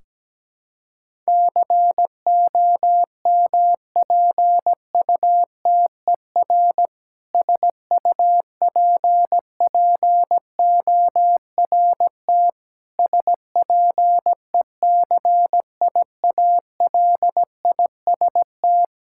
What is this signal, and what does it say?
Morse code, heard as a keyed tone, about 17 wpm, 700 Hz. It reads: COMPUTER SUPPORT SPECIALIST
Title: K1USN SST QSO Practice 17wpm